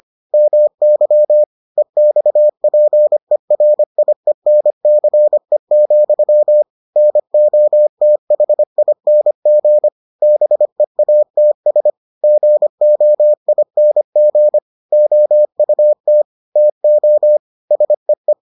Morse code, heard as a keyed tone, about 25 wpm, 600 Hz. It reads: MY EXPERIENCE, NOTHING BEATS GOING OUT TO SEE